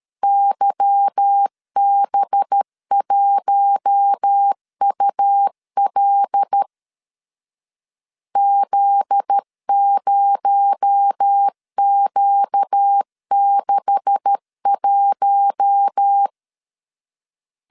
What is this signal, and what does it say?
Morse code, heard as a keyed tone, about 13 wpm, 800 Hz. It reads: YB1UL Z0Q61
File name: CW / DET / 1 9 1500C S B 12wpm 3819